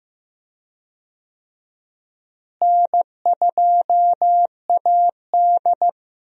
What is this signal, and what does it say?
Morse code, heard as a keyed tone, about 15 wpm, 700 Hz. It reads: N2AD
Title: USA Call Signs - 4 Characters - Repeated 3x - 15wpm